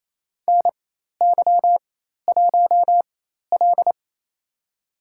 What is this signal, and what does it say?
Morse code, heard as a keyed tone, about 28 wpm, spaced as for 13 wpm, 700 Hz. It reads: NY1L
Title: USA Call Signs - 4 Characters - 4x Character Spacing - 28wpm